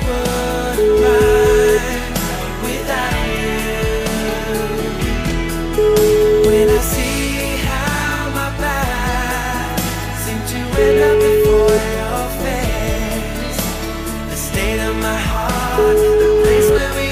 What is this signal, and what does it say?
Ringback tone (Europe, 425 Hz), with noise.